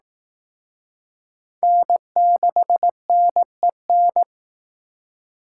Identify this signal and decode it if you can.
Morse code, heard as a keyed tone, about 19 wpm, 700 Hz. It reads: N6NEN